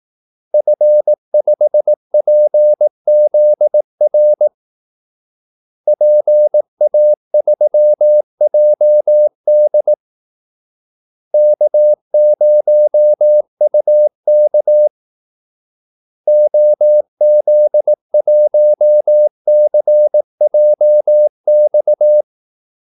Morse code, heard as a keyed tone, about 18 wpm, 600 Hz. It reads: F5PZR PA3JD K0UK OZ1CJX
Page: Kaldesignaler 18 wpm